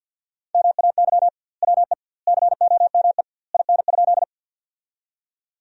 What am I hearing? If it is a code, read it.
Morse code, heard as a keyed tone, about 50 wpm, 700 Hz. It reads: MAY WE COME IN?
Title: 2x Word-Spacing - Sentences from Top 100 Words - 50wpm